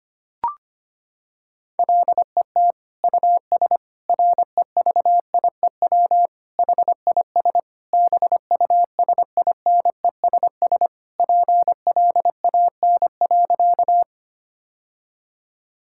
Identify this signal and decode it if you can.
Morse code, heard as a keyed tone, about 25 wpm, 700 Hz. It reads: LET US REVIEW HIS BUSINESS PLAN.
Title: Sentences from Top 1000 Words - Review Part 6 - 25wpm